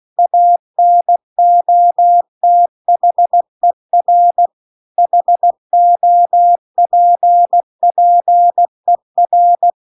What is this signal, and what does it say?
Morse code, heard as a keyed tone, about 16 wpm, 700 Hz. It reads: ANOTHER HOPPER
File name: CW-The-Big-Twoheaded-River-2-16WPM